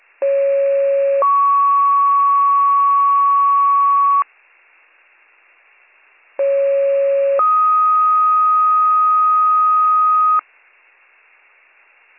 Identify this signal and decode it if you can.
Two-tone page, heard as a one-second tone, then a three-second tone, about 550 then 1120 Hz, about 550 then 1250 Hz. It is heard from 0.2 to 10.4 s.